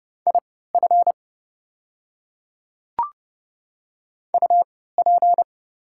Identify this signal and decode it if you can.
Morse code, heard as a keyed tone, about 30 wpm, spaced as for 17 wpm, 700 Hz. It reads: IF UP
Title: 3x Character Spacing - Top 300 Words - 30wpm